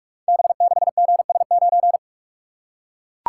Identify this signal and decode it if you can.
Morse code, heard as a keyed tone, about 44 wpm, 700 Hz. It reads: B6GS9
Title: Code Groups - 5 Letter:Numbers - 45wpm